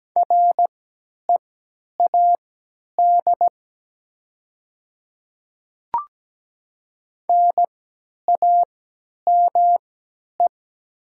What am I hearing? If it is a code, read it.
Morse code, heard as a keyed tone, about 17 wpm, spaced as for 10 wpm, 700 Hz. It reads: READ NAME
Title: 3x Character Spacing - Top 200 Words - 17wpm